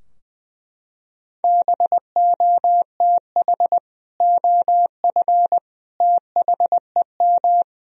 Morse code, heard as a keyed tone, about 20 wpm, 700 Hz. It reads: BOTH OF THEM
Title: Sentences from Top 1000 Words - Review Part 3 - 20wpm